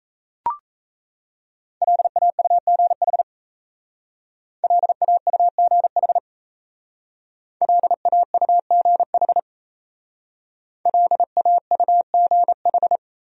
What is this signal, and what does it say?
Morse code, heard as a keyed tone, about 35 wpm, 700 Hz. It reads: LAUGH LAUGH LAUGH LAUGH